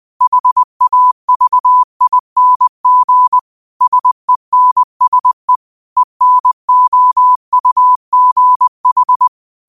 Morse code, heard as a keyed tone, about 20 wpm, 1000 Hz. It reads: HAVING SENSE ENOUGH